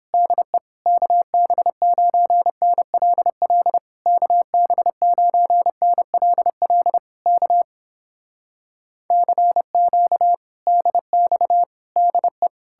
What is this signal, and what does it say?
Morse code, heard as a keyed tone, about 32 wpm, 700 Hz. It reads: DE KB9NLL KB9NLL K CQ DX DE